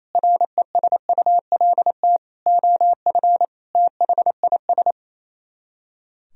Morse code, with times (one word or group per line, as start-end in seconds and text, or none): 0.1-2.2 RESULT
2.5-3.5 OF
3.8-4.9 THIS